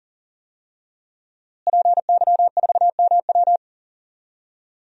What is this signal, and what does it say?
Morse code, heard as a keyed tone, about 40 wpm, 700 Hz. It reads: PY4MW
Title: DXCC-100 Call Signs - 5 Characters - Repeated 3x - 40wpm